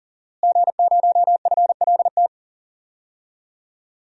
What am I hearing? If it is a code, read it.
Morse code, heard as a keyed tone, about 40 wpm, 700 Hz. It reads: G0FLT
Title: DXCC-100 Call Signs - 5 Characters - Repeated 2x - 40wpm